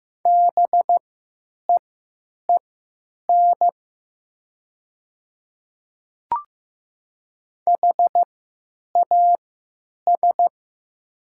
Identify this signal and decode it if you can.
Morse code, heard as a keyed tone, about 15 wpm, spaced as for 9 wpm, 700 Hz. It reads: BEEN HAS
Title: 3x Character Spacing - Top 200 Words - 15wpm